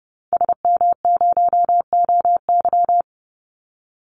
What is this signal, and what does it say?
Morse code, heard as a keyed tone, about 29 wpm, 700 Hz. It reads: SM0OY
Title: Call Signs Repeated 2x - 30wpm